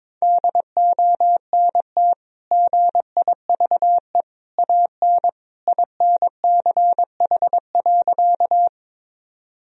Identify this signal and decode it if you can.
Morse code, heard as a keyed tone, about 22 wpm, 700 Hz. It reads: DONT GIVE AN INCH.
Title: Sentences from Top 1000 Words - Review Part 3 - 22wpm